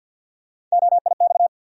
Morse code, heard as a keyed tone, about 50 wpm, 700 Hz. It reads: OIX
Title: USA Call Sign Postfixes - 3 Letters 50wpm